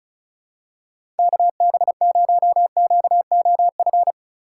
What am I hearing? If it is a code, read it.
Morse code, heard as a keyed tone, about 36 wpm, 700 Hz. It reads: KB0QOF
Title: USA Call Signs - 6 Characters - 4x Character Spacing - 35wpm